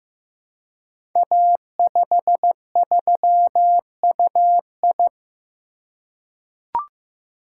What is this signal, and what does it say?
Morse code, heard as a keyed tone, about 15 wpm, 700 Hz. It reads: A53UI
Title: Code Groups - 5 Letter:Numbers - 15wpm